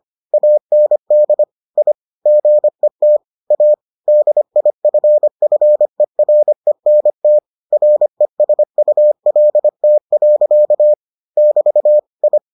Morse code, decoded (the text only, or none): AND I GET A DIFFERENT RESULT. <BT> I